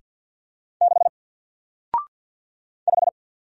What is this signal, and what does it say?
Morse code, heard as a keyed tone, about 49 wpm, 700 Hz. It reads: 6 5
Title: Single Character (Letter:Number:Punctuation:Prosign) Alt 50wpm